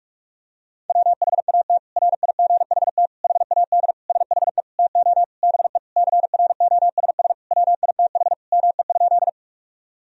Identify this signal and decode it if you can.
Morse code, heard as a keyed tone, about 41 wpm, 700 Hz. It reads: WHAT RIGHT HAD SHE TO BE CROSS WITH ME?
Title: Sentences from Top 1000 Words - Review Part 1 - 45wpm